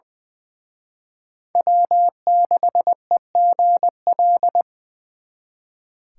Morse code, heard as a keyed tone, about 24 wpm, 700 Hz. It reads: W6EGL
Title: Speed Racing - USA 1x3 Call Signs 20wpm